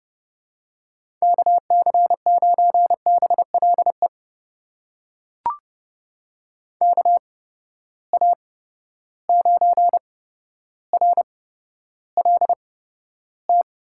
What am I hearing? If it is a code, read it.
Morse code, heard as a keyed tone, about 30 wpm, spaced as for 13 wpm, 700 Hz. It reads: KC9BLE KA9RLT
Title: USA Call Signs - 8x Character Spacing - 30wpm